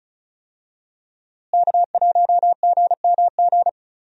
Morse code, heard as a keyed tone, about 34 wpm, 700 Hz. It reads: K1GMG
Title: USA Call Signs - 5 Characters - Repeated 2x - 35wpm